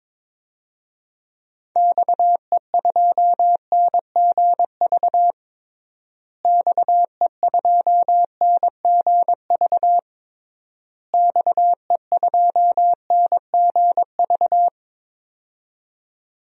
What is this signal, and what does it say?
Morse code, heard as a keyed tone, about 22 wpm, 700 Hz. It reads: XE2NGV XE2NGV XE2NGV
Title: DXCC-100 Call Signs - 6 Characters - Repeated 3x - 22wpm